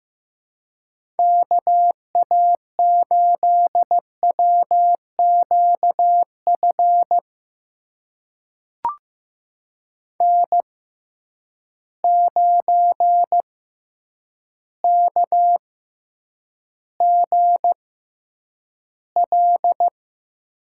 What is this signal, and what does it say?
Morse code, heard as a keyed tone, about 15 wpm, spaced as for 8 wpm, 700 Hz. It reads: KA8WQF N9KGL
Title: USA Call Signs - 6x Character Spacing - 15wpm